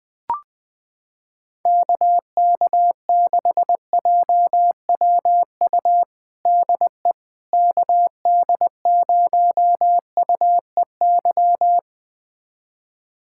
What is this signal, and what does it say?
Morse code, heard as a keyed tone, about 20 wpm, 700 Hz. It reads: KK6JWU DE KD0UEY
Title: USA Callsign DE Callsign 20wpm